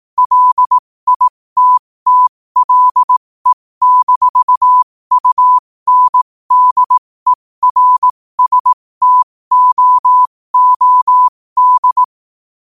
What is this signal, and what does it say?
Morse code, heard as a keyed tone, about 18 wpm, spaced as for 15 wpm, 1000 Hz. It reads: LITTLE-UNDERSTOOD
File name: qotdfiles / 15wpm